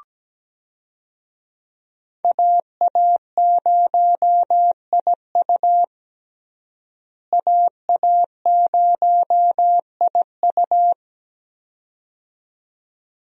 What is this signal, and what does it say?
Morse code, heard as a keyed tone, about 17 wpm, 700 Hz. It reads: AA0IU AA0IU
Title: USA Call Signs - 5 Characters - Repeated 2x - 17wpm